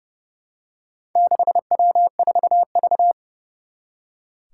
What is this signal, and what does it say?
Morse code, heard as a keyed tone, about 30 wpm, 700 Hz. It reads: 6W4V
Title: Four Challenging Letter-Number Combinations 30wpm